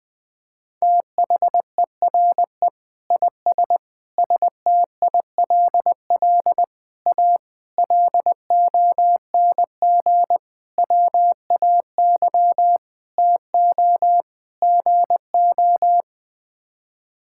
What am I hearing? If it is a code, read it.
Morse code, heard as a keyed tone, about 20 wpm, 700 Hz. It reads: THERE IS STILL A LONG WAY TO GO
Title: Sets of 8 Words - 20wpm